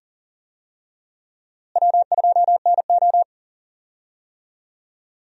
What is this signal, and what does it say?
Morse code, heard as a keyed tone, about 39 wpm, 700 Hz. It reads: W2NO
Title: USA Call Signs - 4 Characters - Repeated 2x - 40wpm